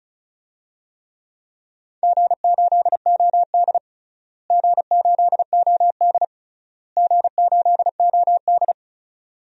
Morse code, heard as a keyed tone, about 34 wpm, 700 Hz. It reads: G8OD G8OD G8OD
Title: DXCC-100 Call Signs - 4 Characters - Repeated 3x - 35wpm